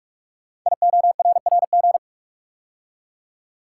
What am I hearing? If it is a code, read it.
Morse code, heard as a keyed tone, about 44 wpm, 700 Hz. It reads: IORRG
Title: Code Groups - 5 Letters - 45wpm